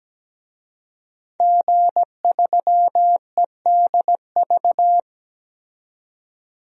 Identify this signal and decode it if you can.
Morse code, heard as a keyed tone, about 17 wpm, 700 Hz. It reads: G3EDV
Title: DXCC-100 Call Signs - 5 Characters - 17wpm